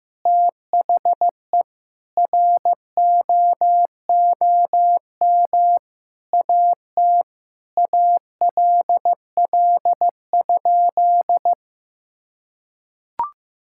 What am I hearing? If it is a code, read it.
Morse code, heard as a keyed tone, about 15 wpm, 700 Hz. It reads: THE ROOM AT ALL?
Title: Sentences from Top 1600 Words 15wpm